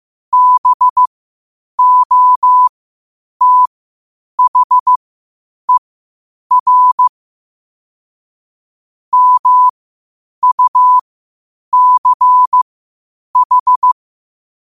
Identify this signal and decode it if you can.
Morse code, heard as a keyed tone, about 15 wpm, spaced as for 8 wpm, 1000 Hz. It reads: BOTHER MUCH
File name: qotdfiles / 8wpm